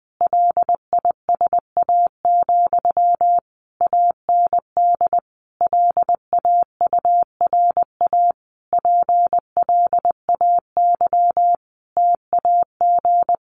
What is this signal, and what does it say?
Morse code, heard as a keyed tone, about 20 wpm, 700 Hz. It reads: LISA, AND LAURA PLAY TAG
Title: Additional Easy Phrases 20wpm